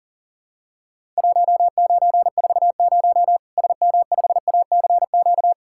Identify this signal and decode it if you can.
Morse code, heard as a keyed tone, about 40 wpm, 700 Hz. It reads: 1940 SM5ACQ
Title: Worked All Europe DX Contest - QTC Exchange - 40wpm